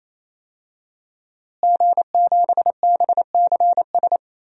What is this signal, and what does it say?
Morse code, heard as a keyed tone, about 28 wpm, 700 Hz. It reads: G7BCS
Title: DXCC-100 Call Signs - 5 Characters - Repeated 3x - 28wpm